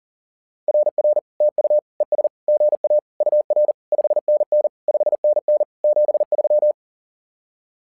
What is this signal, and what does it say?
Morse code, heard as a keyed tone, about 40 wpm, 600 Hz. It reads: RR TU ES GA UR 5NN 5NN 73
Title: SOTA QSO Practice 40wpm